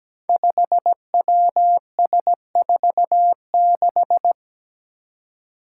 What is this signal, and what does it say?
Morse code, heard as a keyed tone, about 17 wpm, 700 Hz. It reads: 5WS46
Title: Five Challenging Letter-Number Combinations 17wpm